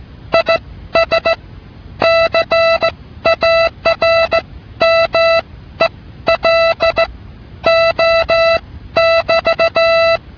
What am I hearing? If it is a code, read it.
Morse code, heard as a keyed tone, about 16 wpm, spaced as for 12 wpm, 660 Hz. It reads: IS CARMELO <BT>
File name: telegrafia / qthname